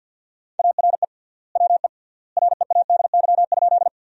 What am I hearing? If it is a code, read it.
Morse code, heard as a keyed tone, about 50 wpm, 700 Hz. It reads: ARE WE READY?
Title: 3x Word-Spacing - Sentences from Top 400 Words - 50wpm